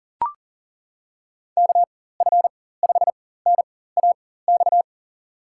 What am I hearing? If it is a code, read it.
Morse code, heard as a keyed tone, about 40 wpm, spaced as for 19 wpm, 700 Hz. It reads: KF5NAX
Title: USA Call Signs - 6 Characters - 4x Character Spacing - 40wpm